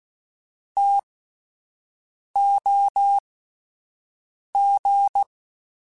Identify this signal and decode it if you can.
Morse code, heard as a keyed tone, about 16 wpm, spaced as for 5 wpm, 800 Hz. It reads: TOG